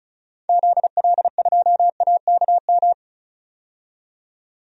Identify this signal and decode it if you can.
Morse code, heard as a keyed tone, about 34 wpm, 700 Hz. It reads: ZL2AKM
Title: DXCC-100 Call Signs - 6 Characters - 35wpm